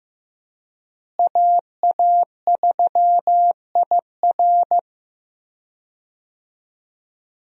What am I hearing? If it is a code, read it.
Morse code, heard as a keyed tone, about 15 wpm, 700 Hz. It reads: AA3IR
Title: USA Call Signs - 5 Characters - Repeated 3x - 15wpm